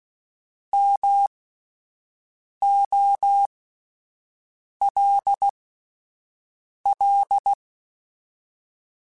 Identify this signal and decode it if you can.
Morse code, heard as a keyed tone, about 16 wpm, spaced as for 5 wpm, 800 Hz. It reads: MOLL
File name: Alphabetical / Alphabetical mp3s / Ms @5